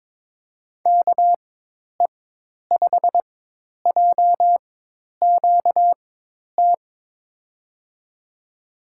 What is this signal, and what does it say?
Morse code, heard as a keyed tone, about 22 wpm, spaced as for 10 wpm, 700 Hz. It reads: KE5JQT
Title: USA Call Signs - 6 Characters - 4x Character Spacing - 22wpm